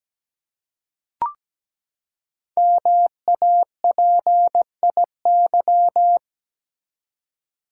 Morse code, heard as a keyed tone, about 17 wpm, 700 Hz. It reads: MAPIY
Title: Code Groups - 5 Letters - 17wpm